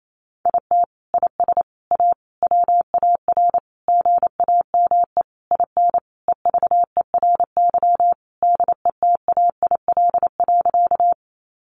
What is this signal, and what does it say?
Morse code, heard as a keyed tone, about 28 wpm, 700 Hz. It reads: IT IS A WAR GAME IN EVERY DETAIL.
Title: Sentences from Top 1500 Words 28wpm